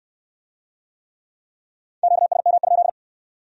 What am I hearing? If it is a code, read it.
Morse code, heard as a keyed tone, about 62 wpm, 700 Hz. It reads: QSM?